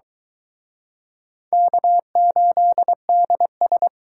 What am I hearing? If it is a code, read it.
Morse code, heard as a keyed tone, about 23 wpm, 700 Hz. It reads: K8DS